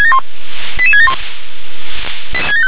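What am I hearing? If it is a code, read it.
Five-tone selective call (ZVEI), heard as quick runs of five tones, tones of about 70 ms. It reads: 88671
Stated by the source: Received on SDR as ZVEI five-tone selective call.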